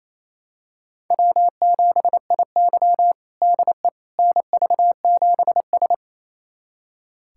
Morse code, heard as a keyed tone, about 28 wpm, 700 Hz. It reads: W7IY DE NV7S
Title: USA Callsign DE Callsign - 4 Characters - 28wpm